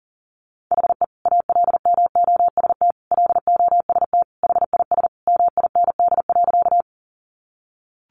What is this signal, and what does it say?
Morse code, heard as a keyed tone, about 37 wpm, 700 Hz. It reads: HE ALMOST LOST HIS MIND.
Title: Sentences from Top 1400 Words 40wpm